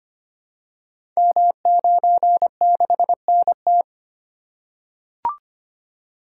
Morse code, heard as a keyed tone, about 25 wpm, 700 Hz. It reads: M96NT
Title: Code Groups - 5 Letter:Numbers - 25wpm